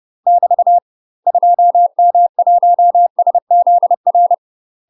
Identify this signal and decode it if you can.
Morse code, heard as a keyed tone, about 30 wpm, 700 Hz. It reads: <BT> 2M1SZR
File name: UK MASTER QSOs TRAINING / 30WPM / UK MASTER QSOs 0045